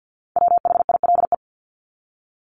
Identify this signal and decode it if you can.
Morse code, heard as a keyed tone, about 50 wpm, 700 Hz. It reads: WHILE
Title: Top 300 Words Repeated 50wpm